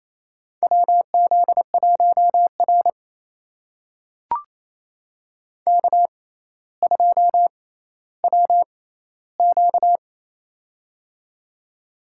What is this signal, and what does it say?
Morse code, heard as a keyed tone, about 27 wpm, spaced as for 14 wpm, 700 Hz. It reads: WZ1R K2WQ
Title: USA Call Signs - 4 Characters - 6x Character Spacing - 28wpm